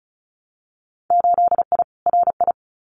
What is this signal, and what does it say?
Morse code, heard as a keyed tone, about 36 wpm, 700 Hz. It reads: 8I RI
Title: Winter Field Day Exchanges Encore 35wpm